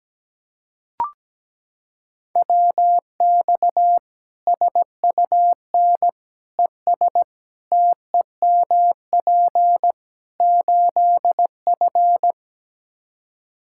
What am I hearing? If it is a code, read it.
Morse code, heard as a keyed tone, about 17 wpm, 700 Hz. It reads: WX SUN ES TEMP 8F
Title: Practice QSO - Exchange 2 - Info - 17wpm